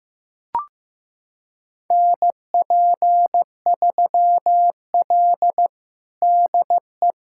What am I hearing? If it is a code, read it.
Morse code, heard as a keyed tone, about 15 wpm, 700 Hz. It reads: NP3L DE
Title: USA Callsign DE Callsign - 4 Characters - 15wpm